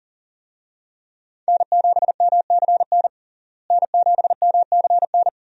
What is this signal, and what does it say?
Morse code, heard as a keyed tone, about 40 wpm, 700 Hz. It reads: N7MCN N7MCN
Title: USA Call Signs - 5 Characters - Repeated 2x - 40wpm